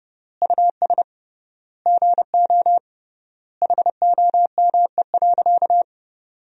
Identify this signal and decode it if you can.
Morse code, heard as a keyed tone, about 32 wpm, 700 Hz. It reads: US GO HOME.
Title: 3x Word-Spacing - Sentences from Top 300 Words - 30wpm